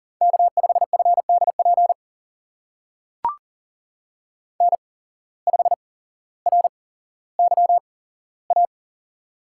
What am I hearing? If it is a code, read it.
Morse code, heard as a keyed tone, about 40 wpm, spaced as for 17 wpm, 700 Hz. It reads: K5FDP N5RYA
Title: USA Call Signs - 5 Characters - 8x Character Spacing - 40wpm